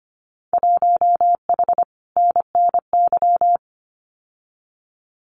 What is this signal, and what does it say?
Morse code, heard as a keyed tone, about 25 wpm, 700 Hz. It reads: 1H NNY
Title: Winter Field Day Exchanges 25wpm